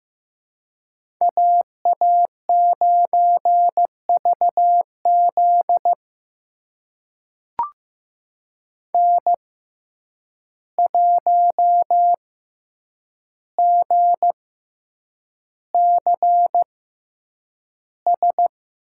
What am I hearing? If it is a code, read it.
Morse code, heard as a keyed tone, about 16 wpm, spaced as for 8 wpm, 700 Hz. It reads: AA9VZ N1GCS
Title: USA Call Signs - 6x Character Spacing - 15wpm